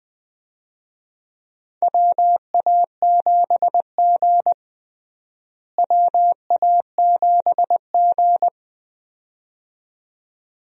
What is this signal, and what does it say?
Morse code, heard as a keyed tone, about 20 wpm, 700 Hz. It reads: WA7G WA7G
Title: USA Call Signs - 4 Characters - Repeated 2x - 20wpm